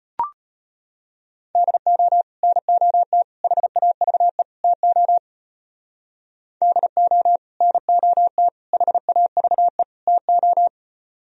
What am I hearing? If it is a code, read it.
Morse code, heard as a keyed tone, about 36 wpm, 700 Hz. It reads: DO NOT HAVE TO DO NOT HAVE TO